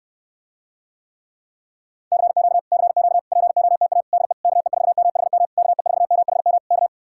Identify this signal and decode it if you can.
Morse code, heard as a keyed tone, about 69 wpm, 700 Hz. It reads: CQ CQ POTA DE KE4MHM KE4MHM K